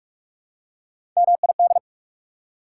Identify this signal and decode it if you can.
Morse code, heard as a keyed tone, about 42 wpm, 690 Hz. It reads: MID
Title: Prefixes and Suffixes 45wpm